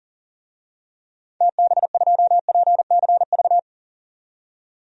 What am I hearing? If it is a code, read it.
Morse code, heard as a keyed tone, about 40 wpm, 700 Hz. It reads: TB2PCV
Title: DXCC-100 Call Signs - 6 Characters - 40wpm